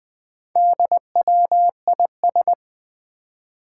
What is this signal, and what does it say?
Morse code, heard as a keyed tone, about 21 wpm, 700 Hz. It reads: DWIS
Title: Four Challenging Letter-Number Combinations 20wpm